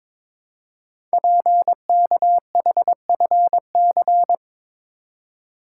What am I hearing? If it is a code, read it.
Morse code, heard as a keyed tone, about 22 wpm, 700 Hz. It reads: PKHFC
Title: Code Groups - 5 Letters - 22wpm